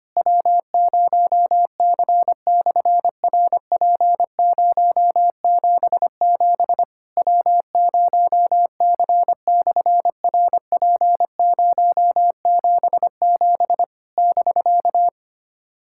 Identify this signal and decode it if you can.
Morse code, heard as a keyed tone, about 24 wpm, 700 Hz. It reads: W0C/RP077 W0C/RP077 <BK>
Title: SOTA QSO Practice 25wpm